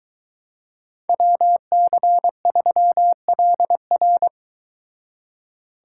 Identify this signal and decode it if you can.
Morse code, heard as a keyed tone, about 22 wpm, 700 Hz. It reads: WC3LR